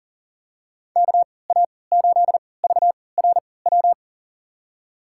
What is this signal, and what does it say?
Morse code, heard as a keyed tone, about 40 wpm, spaced as for 23 wpm, 700 Hz. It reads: KA8VRW
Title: USA Call Signs - 6 Characters - 3x Character Spacing - 40wpm